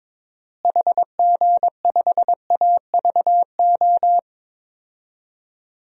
Morse code, heard as a keyed tone, about 22 wpm, 700 Hz. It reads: HG5AVO
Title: DXCC-100 Call Signs - 6 Characters - Repeated 2x - 22wpm